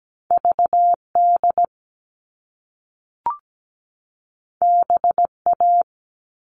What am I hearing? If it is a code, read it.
Morse code, heard as a keyed tone, about 17 wpm, 700 Hz. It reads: VD BA
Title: All Cut Numbers - 2-Digit 17wpm